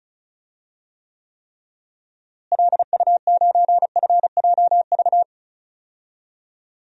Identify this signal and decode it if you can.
Morse code, heard as a keyed tone, about 36 wpm, 700 Hz. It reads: LU9FJV
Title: DXCC-100 Call Signs - 6 Characters - Repeated 3x - 35wpm